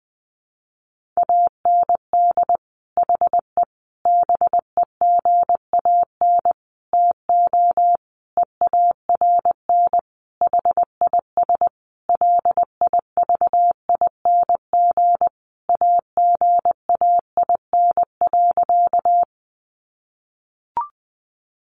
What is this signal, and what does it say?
Morse code, heard as a keyed tone, about 20 wpm, 700 Hz. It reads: AND HE BEGAN TO EARN HIS LIVING AGAIN.
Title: Sentences from Top 1500 Words 20wpm